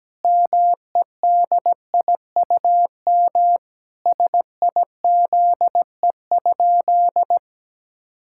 Morse code, heard as a keyed tone, about 17 wpm, 700 Hz. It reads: MEDIUM SIZE?
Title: Sentences from Top 1000 Words - Review Part 6 - 17wpm